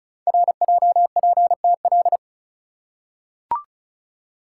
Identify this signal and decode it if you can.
Morse code, heard as a keyed tone, about 36 wpm, 690 Hz. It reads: RJPTL
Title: Code Groups - 5 Letters - 35wpm